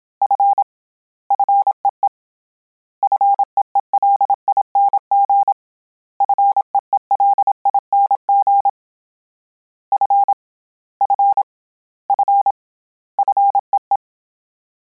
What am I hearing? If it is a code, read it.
Morse code, heard as a keyed tone, about 27 wpm, 800 Hz. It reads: F FEE FEELING FEELING F F F FEE
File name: Combinations / Combinations-mp3s / Combinations@25